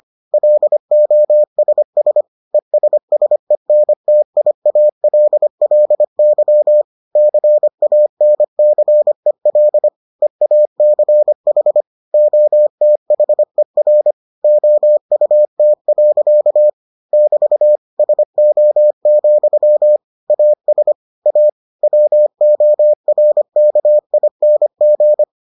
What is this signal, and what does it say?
Morse code, heard as a keyed tone, about 25 wpm, 600 Hz. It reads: LOSS ESSENTIALLY CANCEL EACH OTHER OUT. <BT> SO, AS A WORKING